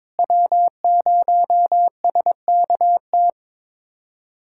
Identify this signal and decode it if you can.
Morse code, heard as a keyed tone, about 22 wpm, 700 Hz. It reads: W0SKT
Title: USA Callsign DE Callsign - 5 Characters - 22wpm